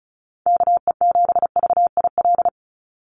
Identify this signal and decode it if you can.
Morse code, heard as a keyed tone, about 34 wpm, 700 Hz. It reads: KE7VIL
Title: USA Call Signs 35wpm